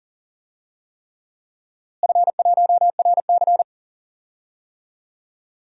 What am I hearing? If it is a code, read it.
Morse code, heard as a keyed tone, about 40 wpm, 700 Hz. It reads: F1RC
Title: DXCC-100 Call Signs - 4 Characters - Repeated 3x - 40wpm